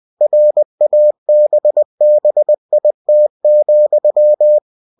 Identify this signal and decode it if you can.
Morse code, heard as a keyed tone, about 20 wpm, 600 Hz. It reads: RABBIT,
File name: cw-books / winnie-the-pooh / winnie-20wpm-page-0084